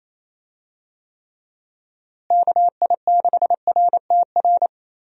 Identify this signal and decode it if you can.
Morse code, heard as a keyed tone, about 28 wpm, 700 Hz. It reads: KI6RTR
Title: USA Call Signs - 6 Characters - Repeated 2x - 28wpm